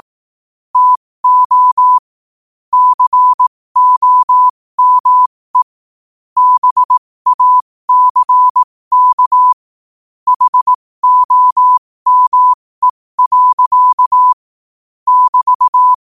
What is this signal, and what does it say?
Morse code, heard as a keyed tone, about 18 wpm, spaced as for 15 wpm, 1000 Hz. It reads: TO COME BACK HOME. <BT>
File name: qotdfiles / 15wpm